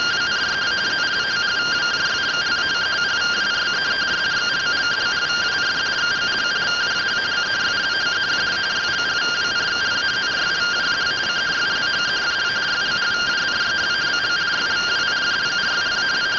Two-tone teleprinter signal, about 170 Hz shift.